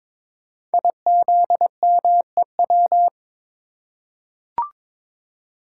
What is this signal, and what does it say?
Morse code, heard as a keyed tone, about 22 wpm, 700 Hz. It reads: IZMEW
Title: Code Groups - 5 Letters - 22wpm